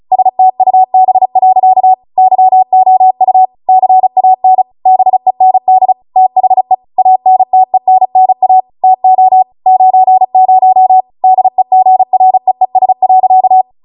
Morse code, heard as a keyed tone, about 36 wpm, 750 Hz. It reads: STUB. YOU CAN BEND THE ANTENNA TO 90 DEGREES.